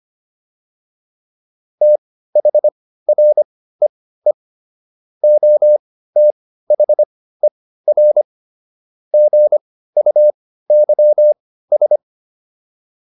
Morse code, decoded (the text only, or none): THREE OTHER GUYS